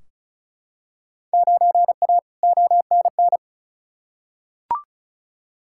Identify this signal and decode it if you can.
Morse code, heard as a keyed tone, about 36 wpm, 700 Hz. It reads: 9A ONN
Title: Field Day Exchanges Encore - 35wpm